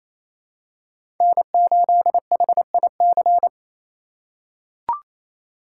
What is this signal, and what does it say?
Morse code, heard as a keyed tone, about 27 wpm, 700 Hz. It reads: N8HIC
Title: USA Call Signs - 5 Characters - 4x Character Spacing - 28wpm